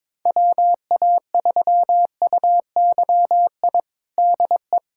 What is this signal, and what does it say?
Morse code, heard as a keyed tone, about 22 wpm, 700 Hz. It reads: WA3UYI DE
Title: USA Callsign DE Callsign - 6 Characters - 22wpm